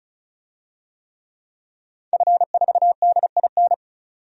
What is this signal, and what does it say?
Morse code, heard as a keyed tone, about 36 wpm, 700 Hz. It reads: F4DIN
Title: DXCC-100 Call Signs - 5 Characters - 35wpm